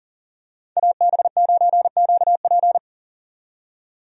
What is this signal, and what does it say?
Morse code, heard as a keyed tone, about 40 wpm, 700 Hz. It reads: AB9QP
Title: USA Call Signs - 5 Characters - 40wpm